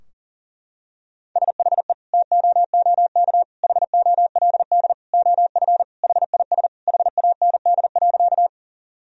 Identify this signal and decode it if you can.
Morse code, heard as a keyed tone, about 40 wpm, 700 Hz. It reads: SHE TOOK HOLD OF HIS HAND.
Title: Sentences from Top 1000 Words - Review Part 2 - 40wpm